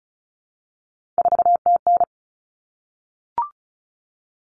Morse code, heard as a keyed tone, about 34 wpm, 700 Hz. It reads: 4TN
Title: All Cut Numbers - 3-Digit 35wpm